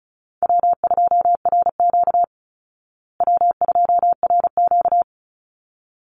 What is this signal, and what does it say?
Morse code, heard as a keyed tone, about 34 wpm, 700 Hz. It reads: W2RQ W2RQ
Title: Easy Call Signs Repeated 2x - 35wpm